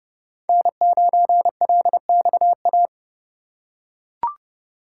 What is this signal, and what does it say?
Morse code, heard as a keyed tone, about 30 wpm, 700 Hz. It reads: N9LXA
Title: USA Call Signs - 5 Characters - 2x Character Spacing - 30wpm